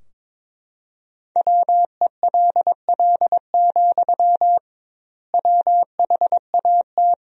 Morse code, heard as a keyed tone, about 22 wpm, 700 Hz. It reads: WELL, WHAT
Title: 2x Word-Spacing - Sentences from Top 200 Words - 22wpm